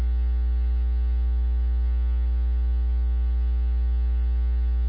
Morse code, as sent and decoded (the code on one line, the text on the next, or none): none